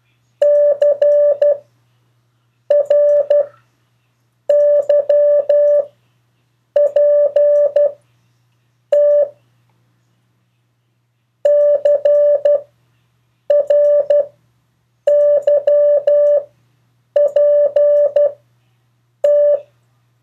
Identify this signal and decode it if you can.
Morse code, heard as a keyed tone, about 12 wpm, spaced as for 6 wpm, 580 Hz. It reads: CRYPT CRYPT
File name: 2of4-words-1